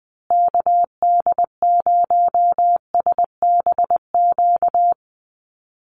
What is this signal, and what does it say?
Morse code, heard as a keyed tone, about 20 wpm, 700 Hz. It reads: KD0SBQ
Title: USA Call Signs Repeated 3x 20wpm